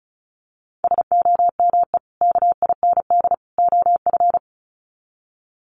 Morse code, heard as a keyed tone, about 34 wpm, 700 Hz. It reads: SOME KIND OF
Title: Sets of 3 Words 35wpm